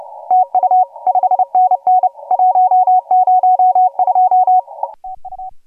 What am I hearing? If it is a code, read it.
Morse code, heard as a keyed tone, about 36 wpm, 750 Hz. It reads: TU 5NN 102 TU
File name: AUDIO2018 / HG1G CW